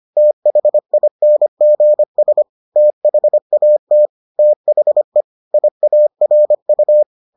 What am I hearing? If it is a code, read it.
Morse code, heard as a keyed tone, about 25 wpm, 600 Hz. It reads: THINGS THAT THE IARU